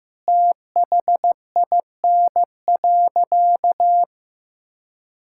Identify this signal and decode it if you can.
Morse code, heard as a keyed tone, about 16 wpm, 700 Hz. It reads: THIN.
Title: Sentences from Top 1600 Words 15wpm